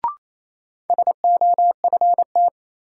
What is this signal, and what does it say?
Morse code, heard as a keyed tone, about 27 wpm, 700 Hz. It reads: SOFT
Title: Sentences from Top 1500 Words - New Word Drill Alt - 28wpm